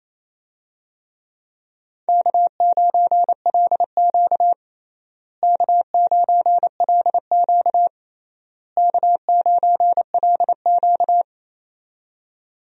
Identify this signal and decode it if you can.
Morse code, heard as a keyed tone, about 29 wpm, 700 Hz. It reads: K9LQ K9LQ K9LQ
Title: USA Call Signs - 4 Characters - Repeated 3x - 28wpm